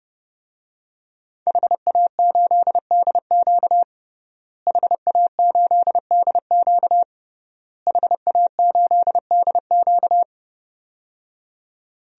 Morse code, heard as a keyed tone, about 30 wpm, 700 Hz. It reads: HA8DQ HA8DQ HA8DQ
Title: DXCC-100 Call Signs - 5 Characters - Repeated 3x - 30wpm